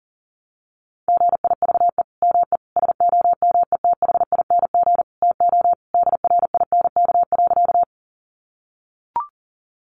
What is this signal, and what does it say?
Morse code, heard as a keyed tone, about 40 wpm, 700 Hz. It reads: GIVE ME SOMETHING TO DRINK.
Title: Sentences from Top 1500 Words 40wpm